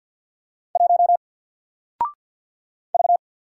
Morse code, heard as a keyed tone, about 48 wpm, 700 Hz. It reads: 1 V
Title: Single Character (Letter:Number:Punctuation:Prosign) 50wpm